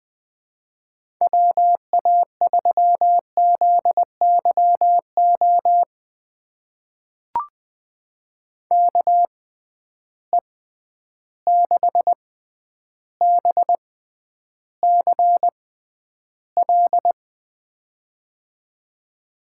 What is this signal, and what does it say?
Morse code, heard as a keyed tone, about 21 wpm, spaced as for 10 wpm, 700 Hz. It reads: WA3ZYO KE6BCL
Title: USA Call Signs - 6x Character Spacing - 20wpm